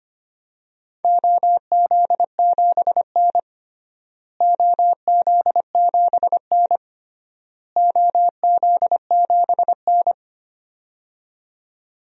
Morse code, heard as a keyed tone, about 25 wpm, 700 Hz. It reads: OZ7N OZ7N OZ7N
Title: DXCC-100 Call Signs - 4 Characters - Repeated 3x - 25wpm